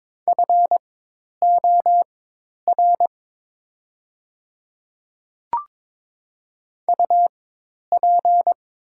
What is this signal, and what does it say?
Morse code, heard as a keyed tone, about 22 wpm, spaced as for 10 wpm, 700 Hz. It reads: FOR UP
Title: 4x Character Spacing - Top 100 Words - 22wpm